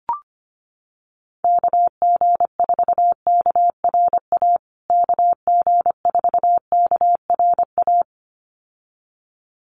Morse code, heard as a keyed tone, about 25 wpm, 700 Hz. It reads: KG4KRA KG4KRA
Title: USA Call Signs Repeated 2x 25wpm